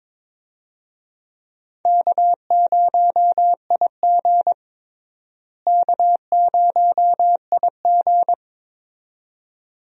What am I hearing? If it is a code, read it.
Morse code, heard as a keyed tone, about 22 wpm, 700 Hz. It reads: K0IG K0IG
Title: USA Call Signs - 4 Characters - Repeated 2x - 22wpm